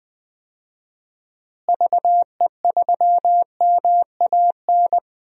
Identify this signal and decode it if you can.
Morse code, heard as a keyed tone, about 20 wpm, 700 Hz. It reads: VE3MAN
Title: DXCC-100 Call Signs - 6 Characters - Repeated 2x - 20wpm